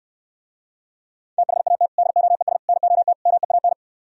Morse code, heard as a keyed tone, about 67 wpm, 700 Hz. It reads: THAT DOES NOT MEAN